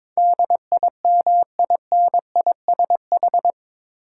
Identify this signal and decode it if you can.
Morse code, heard as a keyed tone, about 22 wpm, 700 Hz. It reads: DIMINISH
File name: practice / 22wpm / Eight Letter Words - 22wpm